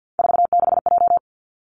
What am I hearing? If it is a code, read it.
Morse code, heard as a keyed tone, about 50 wpm, 700 Hz. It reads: 46J
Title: Three Challenging Letter-Number Combinations 50wpm